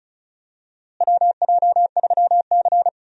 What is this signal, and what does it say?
Morse code, heard as a keyed tone, about 36 wpm, 700 Hz. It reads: WJ3C
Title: USA Call Signs - 4 Characters - Repeated 2x - 35wpm